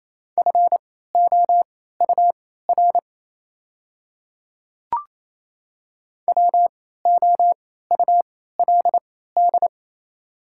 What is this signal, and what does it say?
Morse code, heard as a keyed tone, about 28 wpm, spaced as for 16 wpm, 700 Hz. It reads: FOUR WOULD
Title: 3x Character Spacing - Top 300 Words - 28wpm